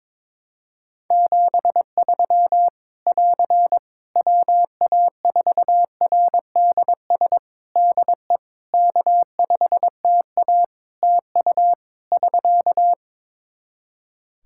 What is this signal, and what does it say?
Morse code, heard as a keyed tone, about 21 wpm, 700 Hz. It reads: 73 <AR> WA4RDS DE K5TA TU <SK>
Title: Practice QSO - Exchange 4 - Ending - 22wpm